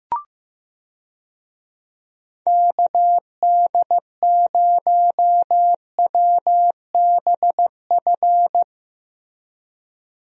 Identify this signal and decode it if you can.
Morse code, heard as a keyed tone, about 16 wpm, 700 Hz. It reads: KD0WBF
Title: USA Call Signs - 6 Characters - Repeated 3x - 15wpm